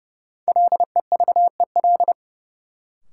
Morse code, heard as a keyed tone, about 29 wpm, 700 Hz. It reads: LEVEL
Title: Sentences from Top 700 Words - New Word Drill - 30wpm